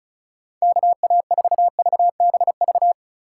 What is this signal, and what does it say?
Morse code, heard as a keyed tone, about 36 wpm, 700 Hz. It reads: KA4VBV
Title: USA Call Signs - 6 Characters - 4x Character Spacing - 35wpm